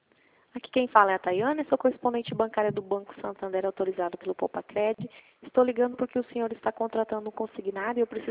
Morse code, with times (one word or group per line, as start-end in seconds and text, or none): none